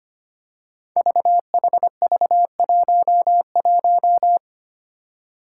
Morse code, heard as a keyed tone, about 25 wpm, 700 Hz. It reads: VHV11
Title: Five Challenging Letter-Number Combinations 25wpm